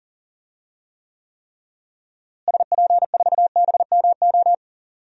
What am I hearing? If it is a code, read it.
Morse code, heard as a keyed tone, about 40 wpm, 700 Hz. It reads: SP4BMO
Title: DXCC-100 Call Signs - 6 Characters - Repeated 3x - 40wpm